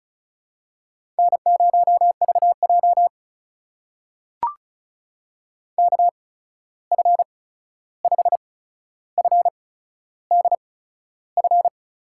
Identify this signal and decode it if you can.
Morse code, heard as a keyed tone, about 36 wpm, spaced as for 13 wpm, 700 Hz. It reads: N0VJ KF5FDF
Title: USA Call Signs - 8x Character Spacing - 35wpm